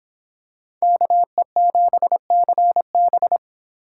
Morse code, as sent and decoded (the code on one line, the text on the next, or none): -.- . --... -.-. -...
KE7CB